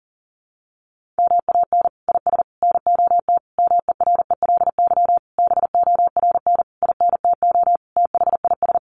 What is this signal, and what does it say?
Morse code, heard as a keyed tone, about 40 wpm, 700 Hz. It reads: MAN IS NOT MERELY BORN INTO THIS
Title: Sentences from Top 1400 Words 40wpm